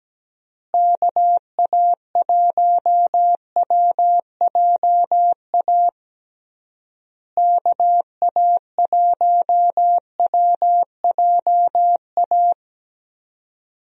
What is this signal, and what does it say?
Morse code, heard as a keyed tone, about 17 wpm, 700 Hz. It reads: KA1WJA KA1WJA
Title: USA Call Signs - 6 Characters - Repeated 3x - 17wpm